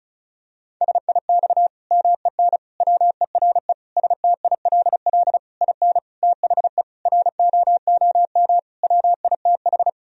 Morse code, heard as a keyed tone, about 36 wpm, 700 Hz. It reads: SIX MEN WERE STILL IN THE ROOM WITH
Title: Sentences from Top 1000 Words - Review Part 2 - 35wpm